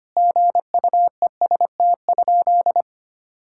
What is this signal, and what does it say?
Morse code, heard as a keyed tone, about 25 wpm, 700 Hz. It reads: GUEST?
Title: Sentences from Top 1600 Words 25wpm